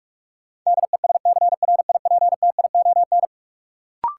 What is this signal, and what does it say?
Morse code, heard as a keyed tone, about 46 wpm, 700 Hz. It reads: DESCRIPTION
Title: Sentences from Top 1300 Words - New Word Drill Alt - 45wpm